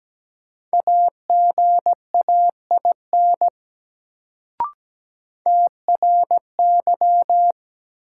Morse code, heard as a keyed tone, about 17 wpm, 700 Hz. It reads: AGAIN TRY
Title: Sentences from Top 200 Words - New Word Drill - 17wpm